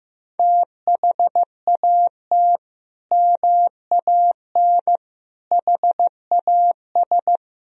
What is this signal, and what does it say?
Morse code, heard as a keyed tone, about 16 wpm, 700 Hz. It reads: THAT MAN HAS
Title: Sentences from Top 1000 Words - Review Part 1 - 15wpm